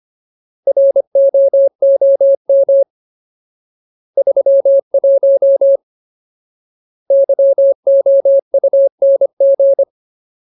Morse code, heard as a keyed tone, about 25 wpm, 550 Hz. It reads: ROOM 31 YOUNG
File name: Ultimate Word List / 25WPM Ultimate